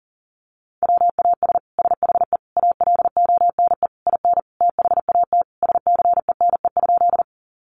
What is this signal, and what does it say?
Morse code, heard as a keyed tone, about 40 wpm, 700 Hz. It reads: WAS SHE ALONE IN THAT SCENE?
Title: Sentences from Top 1100 Words 40wpm